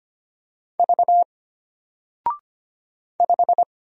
Morse code, heard as a keyed tone, about 25 wpm, 700 Hz. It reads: V 5
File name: Single-Letter-Number-25wpm